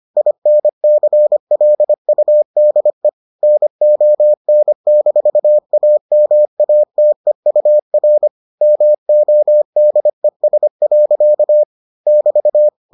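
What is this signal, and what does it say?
Morse code, heard as a keyed tone, about 25 wpm, 600 Hz. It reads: INCLUDE NON-AMATEUR MODES. <BT>
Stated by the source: Morse edition of the Foundations of Amateur Radio episodes.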